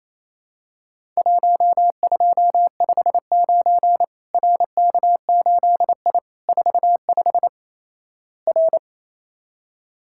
Morse code, heard as a keyed tone, about 28 wpm, 700 Hz. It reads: 1259 RK8I 45 R
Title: Worked All Europe DX Contest - QTC Exchange - 28wpm